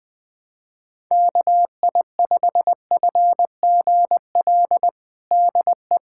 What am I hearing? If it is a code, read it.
Morse code, heard as a keyed tone, about 20 wpm, 700 Hz. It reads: KI5FGL DE
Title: USA Callsign DE Callsign - 6 Characters - 20wpm